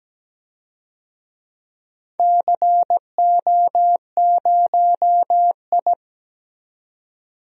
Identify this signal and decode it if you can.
Morse code, heard as a keyed tone, about 17 wpm, 700 Hz. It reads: CO0I
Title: DXCC-100 Call Signs - 4 Characters - 17wpm